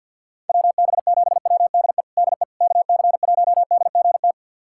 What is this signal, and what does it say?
Morse code, heard as a keyed tone, about 52 wpm, 700 Hz. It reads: WB7WDE DE KC1DGT
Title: USA Callsign DE Callsign - 6 Characters - 50wpm